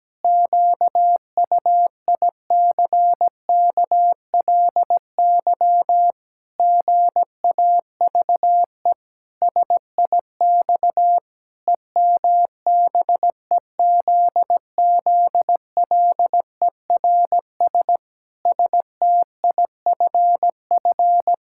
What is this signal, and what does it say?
Morse code, heard as a keyed tone, about 17 wpm, 700 Hz. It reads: QUICKLY GAVE SIX EMBEZZLERS STIFF
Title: Pangram Challenge v2 - 17wpm